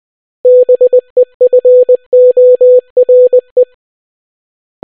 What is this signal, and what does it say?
Morse code, heard as a keyed tone, about 20 wpm, 500 Hz. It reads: BEFORE